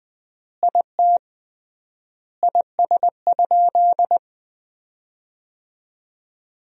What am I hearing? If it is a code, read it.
Morse code, heard as a keyed tone, about 21 wpm, 700 Hz. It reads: IT IS?
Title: 3x Word-Spacing - Sentences from Top 100 Words - 20wpm